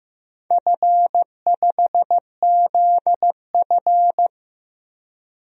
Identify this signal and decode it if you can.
Morse code, heard as a keyed tone, about 15 wpm, 700 Hz. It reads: F5ZF
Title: DXCC-100 Call Signs - 4 Characters - 15wpm